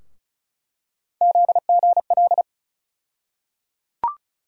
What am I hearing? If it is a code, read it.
Morse code, heard as a keyed tone, about 36 wpm, 700 Hz. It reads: ZGL
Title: USA Call Sign Postfixes - 3 Letters 35wpm